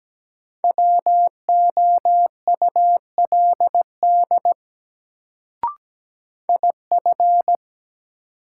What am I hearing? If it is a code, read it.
Morse code, heard as a keyed tone, about 17 wpm, 700 Hz. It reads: WOULD IF
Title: Sets of 2 Words - New Word Drill - 17wpm